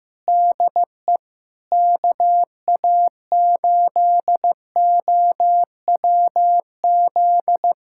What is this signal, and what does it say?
Morse code, heard as a keyed tone, about 15 wpm, 700 Hz. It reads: DE KA8OWZ
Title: USA Callsign DE Callsign - 6 Characters - 15wpm